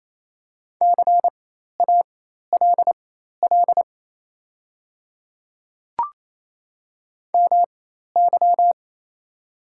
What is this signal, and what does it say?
Morse code, heard as a keyed tone, about 27 wpm, spaced as for 13 wpm, 690 Hz. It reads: CALL MY
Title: 4x Character Spacing - Top 100 Words - 28wpm